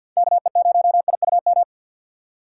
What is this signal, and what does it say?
Morse code, heard as a keyed tone, about 52 wpm, 700 Hz. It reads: KE0IUM
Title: USA Call Signs - 2x Character Spacing - 50wpm